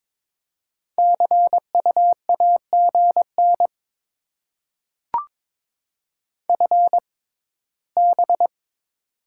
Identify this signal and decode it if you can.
Morse code, heard as a keyed tone, about 21 wpm, spaced as for 16 wpm, 700 Hz. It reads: CUAGN FB
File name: Top 20 QSO Elements - 6x Character Spacing - 22wpm